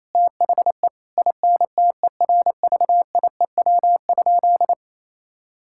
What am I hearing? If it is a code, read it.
Morse code, heard as a keyed tone, about 28 wpm, 700 Hz. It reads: THE INTERVIEW?
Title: Sentences from Top 1600 Words 28wpm